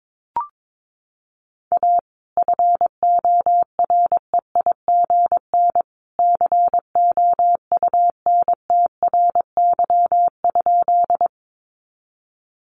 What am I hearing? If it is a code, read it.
Morse code, heard as a keyed tone, about 22 wpm, 700 Hz. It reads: A FOREIGN COUNTRY?
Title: Sentences from Top 1400 Words 22wpm